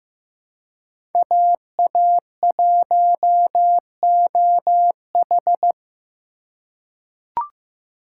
Morse code, heard as a keyed tone, about 15 wpm, 700 Hz. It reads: AA1OH
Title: USA Call Signs - 2x Character Spacing - 15wpm